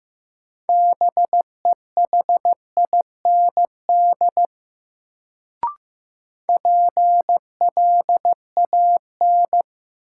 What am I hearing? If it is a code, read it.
Morse code, heard as a keyed tone, about 15 wpm, 700 Hz. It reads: BEHIND PLAN
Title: Sentences from Top 500 Words - New Word Drill - 15wpm